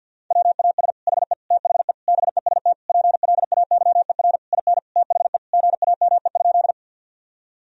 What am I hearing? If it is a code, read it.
Morse code, heard as a keyed tone, about 50 wpm, 700 Hz. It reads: WAS HE THE BEST PLAYER IN THE GAME?
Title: Sentences from Top 1000 Words - Review Part 4 - 50wpm